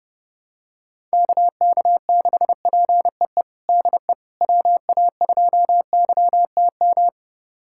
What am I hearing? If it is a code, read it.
Morse code, heard as a keyed tone, about 30 wpm, 700 Hz. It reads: KK6PEE DE WA2YTM
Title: USA Callsign DE Callsign - 6 Characters - 30wpm